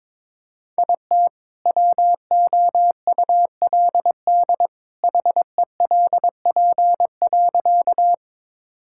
Morse code, heard as a keyed tone, about 21 wpm, 700 Hz. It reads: IT WOULD HELP.
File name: practice / 22wpm / Sentences with 2-Word Phrases - 22wpm